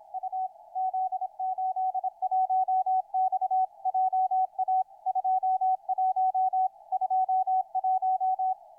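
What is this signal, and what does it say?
Morse code, heard as a keyed tone, about 26 wpm, 750 Hz, with radio noise. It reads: U Z81X JA 21 21
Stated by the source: Morse code heard on the 160 m band.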